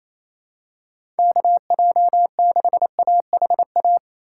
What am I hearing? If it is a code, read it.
Morse code, heard as a keyed tone, about 28 wpm, 700 Hz. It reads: KJ6AHA
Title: USA Call Signs - 6 Characters - 4x Character Spacing - 28wpm